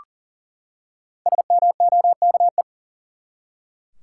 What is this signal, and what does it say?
Morse code, heard as a keyed tone, about 40 wpm, 700 Hz. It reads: SMOKE
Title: Five Letter Words - 40wpm